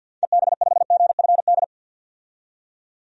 Morse code, heard as a keyed tone, about 48 wpm, 700 Hz. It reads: EB5GFD
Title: DXCC-100 Call Signs - 6 Characters - 50wpm